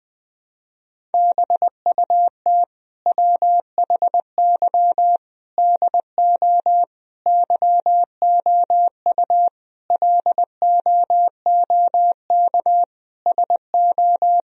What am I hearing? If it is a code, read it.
Morse code, heard as a keyed tone, about 21 wpm, 700 Hz. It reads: BUT WHY DO YOU LOOK SO
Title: Sentences from Top 1000 Words - Review Part 1 - 20wpm